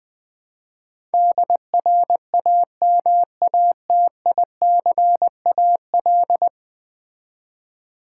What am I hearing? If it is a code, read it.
Morse code, heard as a keyed tone, about 20 wpm, 700 Hz. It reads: DRAMATICAL
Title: Build A Word - 5 Letters At A Time - 12-Letter Words - 20wpm